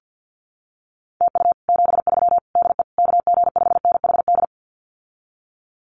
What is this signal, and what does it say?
Morse code, heard as a keyed tone, about 50 wpm, 700 Hz. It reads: TU 73 DE KG5NHD
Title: Common QSO Phrases Encore 50wpm